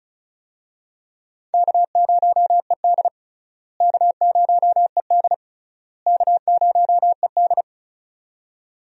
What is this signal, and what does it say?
Morse code, heard as a keyed tone, about 36 wpm, 700 Hz. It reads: K0ED K0ED K0ED
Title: USA Call Signs - 4 Characters - Repeated 3x - 35wpm